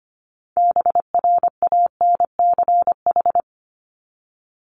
Morse code, heard as a keyed tone, about 25 wpm, 700 Hz. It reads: BRANCH
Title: Top 1751-2000 Words 25wpm